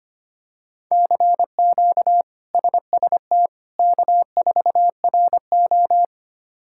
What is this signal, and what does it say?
Morse code, heard as a keyed tone, about 25 wpm, 700 Hz. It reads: CQ SST K4RO
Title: K1USN SST QSO Practice - 25wpm